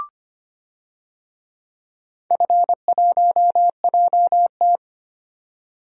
Morse code, heard as a keyed tone, about 25 wpm, 700 Hz. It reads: F1JT
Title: DXCC-100 Call Signs - 4 Characters - 25wpm